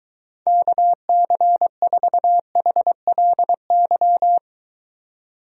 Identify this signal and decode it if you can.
Morse code, heard as a keyed tone, about 23 wpm, 700 Hz. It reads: KC4HLY